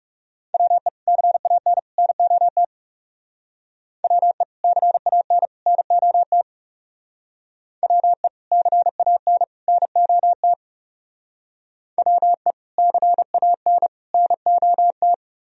Morse code, heard as a keyed tone, about 37 wpm, 700 Hz. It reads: WE CAN NOT WE CAN NOT WE CAN NOT WE CAN NOT
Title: Speed-Racing - Sets of 3 Words 30wpm